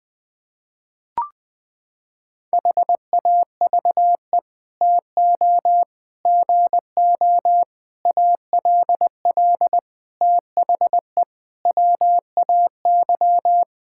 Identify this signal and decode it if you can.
Morse code, heard as a keyed tone, about 21 wpm, 700 Hz. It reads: HAVE TO GO ALL THE WAY
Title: Sets of 8 Words - 20wpm